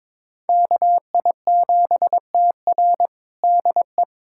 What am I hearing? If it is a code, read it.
Morse code, heard as a keyed tone, about 21 wpm, 700 Hz. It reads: KI7TR DE
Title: USA Callsign DE Callsign - 5 Characters - 22wpm